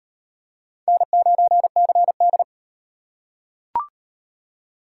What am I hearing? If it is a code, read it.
Morse code, heard as a keyed tone, about 38 wpm, 700 Hz. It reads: N9CD